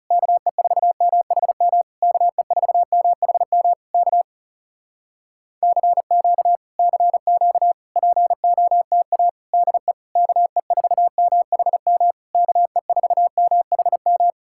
Morse code, heard as a keyed tone, about 37 wpm, 700 Hz. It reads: KE4MHM KE4MHM K CQ CQ POTA DE KE4MHM KE4MHM
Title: Speed-Racing Common QSO Phrases Encore - 30wpm